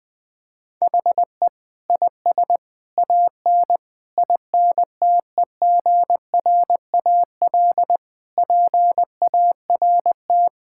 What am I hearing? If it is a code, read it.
Morse code, heard as a keyed tone, about 20 wpm, 700 Hz. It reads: HE IS AN INTEGRAL PART
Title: Sentences with 5-Word Phrases - 20wpm